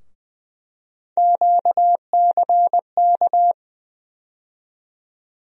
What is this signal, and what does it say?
Morse code, heard as a keyed tone, about 20 wpm, 700 Hz. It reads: QCK
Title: USA Call Sign Postfixes - 3 Letters 20wpm